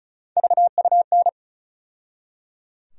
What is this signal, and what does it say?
Morse code, heard as a keyed tone, about 34 wpm, 700 Hz. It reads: VUN
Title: USA Call Sign Postfixes - 3 Letters 35wpm